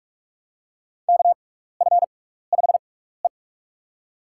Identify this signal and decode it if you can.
Morse code, heard as a keyed tone, about 41 wpm, spaced as for 15 wpm, 700 Hz. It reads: KF5E
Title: USA Call Signs - 4 Characters - 6x Character Spacing - 45wpm